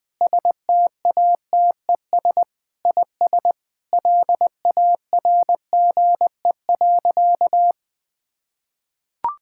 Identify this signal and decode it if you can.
Morse code, heard as a keyed tone, about 20 wpm, 700 Hz. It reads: STATES IS LARGE.
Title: Sentences with 4-Word Phrases - Spoken Beginning - 20wpm